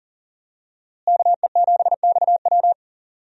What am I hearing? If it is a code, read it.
Morse code, heard as a keyed tone, about 40 wpm, 700 Hz. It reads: KE7XW
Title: USA Call Signs - 5 Characters - Repeated 2x - 40wpm